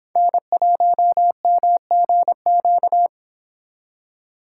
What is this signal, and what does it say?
Morse code, heard as a keyed tone, about 26 wpm, 700 Hz. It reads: N1MGQ